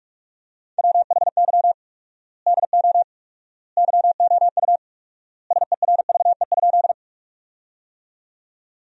Morse code, heard as a keyed tone, about 46 wpm, 700 Hz. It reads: WHY DO YOU SERVE?
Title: 4x Word-Spacing - Sentences from Top 400 Words - 45wpm